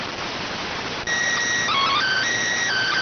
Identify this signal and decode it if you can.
SSTV (slow-scan television), PD 290.